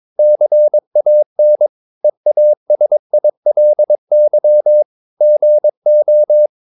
Morse code, heard as a keyed tone, about 22 wpm, 600 Hz. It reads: CAN EASILY GO
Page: Romanen Never af Ken Follet på engelsk.